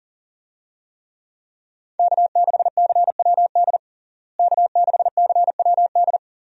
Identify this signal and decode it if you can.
Morse code, heard as a keyed tone, about 40 wpm, 700 Hz. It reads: K6CWD K6CWD
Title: USA Call Signs - 5 Characters - Repeated 2x - 40wpm